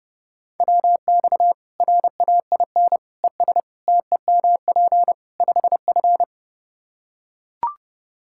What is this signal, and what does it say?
Morse code, heard as a keyed tone, about 30 wpm, 700 Hz. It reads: WX RAIN ES TEMP 5F
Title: K1USN SST QSO Practice 30wpm